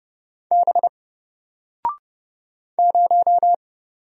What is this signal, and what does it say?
Morse code, heard as a keyed tone, about 30 wpm, 700 Hz. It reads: B 0
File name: Single-Letter-Number-30wpm